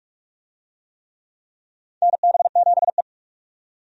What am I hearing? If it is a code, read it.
Morse code, heard as a keyed tone, about 46 wpm, 700 Hz. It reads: NB7E
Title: USA Call Signs - 4 Characters - 45wpm